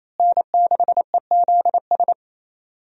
Morse code, heard as a keyed tone, about 28 wpm, 700 Hz. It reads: N6EZS
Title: USA Call Signs - 5 Characters - Repeated 2x - 28wpm